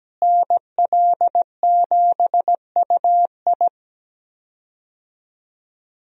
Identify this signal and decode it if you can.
Morse code, heard as a keyed tone, about 17 wpm, 700 Hz. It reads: NL7UI
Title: USA Call Signs - 5 Characters - Repeated 2x - 17wpm